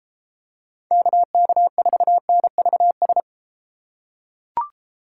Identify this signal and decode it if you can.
Morse code, heard as a keyed tone, about 31 wpm, 700 Hz. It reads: KK4NVS